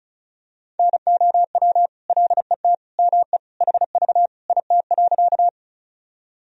Morse code, heard as a keyed tone, about 34 wpm, 700 Hz. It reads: NOW LET ME HV IT.
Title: Sentences from Top 300 Words and Abbreviations 35wpm